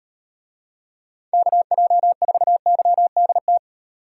Morse code, heard as a keyed tone, about 38 wpm, 700 Hz. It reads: KJ4YDT